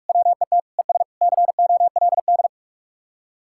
Morse code, heard as a keyed tone, about 59 wpm, 700 Hz. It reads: WET ES COLD